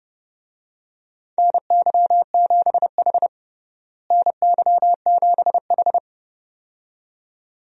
Morse code, heard as a keyed tone, about 30 wpm, 700 Hz. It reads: NY7H NY7H
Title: USA Call Signs - 4 Characters - Repeated 2x - 30wpm